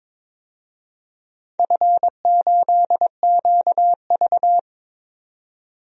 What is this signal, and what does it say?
Morse code, heard as a keyed tone, about 21 wpm, 700 Hz. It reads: F8QV
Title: DXCC-100 Call Signs - 4 Characters - 22wpm